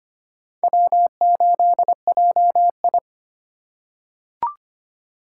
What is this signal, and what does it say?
Morse code, heard as a keyed tone, about 25 wpm, 700 Hz. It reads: W8JI
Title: USA Call Signs - 4 Characters - 4x Character Spacing - 25wpm